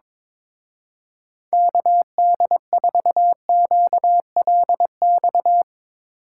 Morse code, heard as a keyed tone, about 22 wpm, 700 Hz. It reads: KD4QLX
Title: Speed Racing - USA 2x3 Call Signs 22wpm